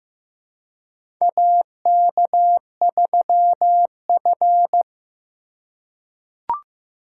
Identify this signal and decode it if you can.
Morse code, heard as a keyed tone, about 15 wpm, 700 Hz. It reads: AK3F
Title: USA Call Signs - 4 Characters - 3x Character Spacing - 15wpm